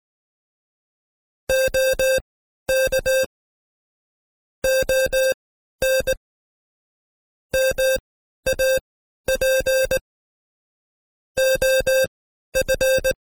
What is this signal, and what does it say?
Morse code, heard as a keyed tone, about 19 wpm, spaced as for 12 wpm, 530 Hz. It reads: OK ON MAP OF
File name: Code Quick / AudioDownld2 / Seg132-Stepper5 11WPM